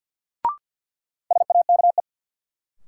Morse code, heard as a keyed tone, about 50 wpm, 700 Hz. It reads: SAKE
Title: Sets of 4 Words - New Word Drill - 50wpm